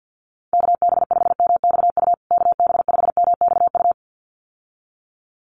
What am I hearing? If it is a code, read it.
Morse code, heard as a keyed tone, about 52 wpm, 700 Hz. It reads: KB5MXU KB5MXU
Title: USA Call Signs Repeated 2x 50wpm